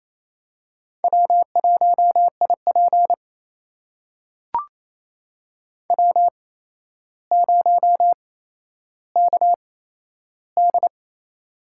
Morse code, heard as a keyed tone, about 27 wpm, spaced as for 12 wpm, 700 Hz. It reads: W1IP W0KD
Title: USA Call Signs - 4 Characters - 8x Character Spacing - 28wpm